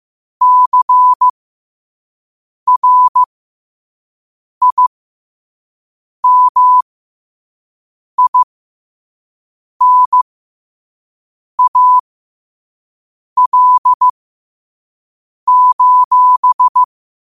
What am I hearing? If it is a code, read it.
Morse code, heard as a keyed tone, about 15 wpm, spaced as for 5 wpm, 1000 Hz. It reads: CRIMINAL:
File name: qotdfiles / 5wpm